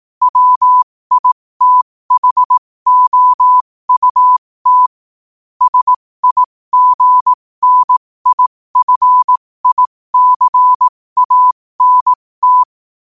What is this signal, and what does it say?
Morse code, heard as a keyed tone, about 18 wpm, spaced as for 15 wpm, 1000 Hz. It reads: WITHOUT SIGNIFICANT
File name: qotdfiles / 15wpm